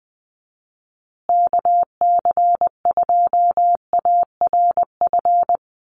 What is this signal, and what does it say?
Morse code, heard as a keyed tone, about 20 wpm, 700 Hz. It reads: KC2ARF
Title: USA Call Signs Repeated 2x 20wpm